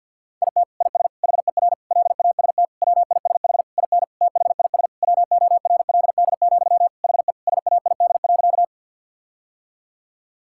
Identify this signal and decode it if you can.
Morse code, heard as a keyed tone, about 49 wpm, 700 Hz. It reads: IT IS HER LAST WISH IN THIS WORLD, HE SAID.
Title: Sentences from Top 1600 Words 50wpm